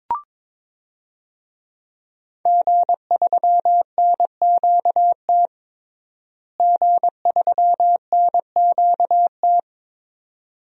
Morse code, heard as a keyed tone, about 22 wpm, 700 Hz. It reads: G3NQT G3NQT
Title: DXCC-100 Call Signs - 5 Characters - Repeated 2x - 22wpm